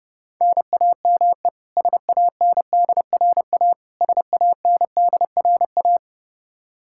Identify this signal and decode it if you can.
Morse code, heard as a keyed tone, about 30 wpm, 700 Hz. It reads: NAME SANDRA SANDRA
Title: Practice QSO - Exchange 1 - Beginning - 30wpm